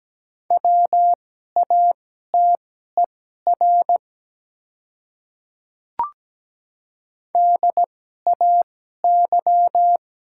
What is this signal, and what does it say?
Morse code, heard as a keyed tone, about 17 wpm, spaced as for 12 wpm, 700 Hz. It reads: WATER DAY
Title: 2x Character Spacing - Top 200 Words - 17wpm